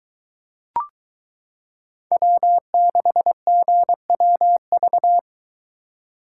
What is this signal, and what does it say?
Morse code, heard as a keyed tone, about 22 wpm, 700 Hz. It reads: W6GWV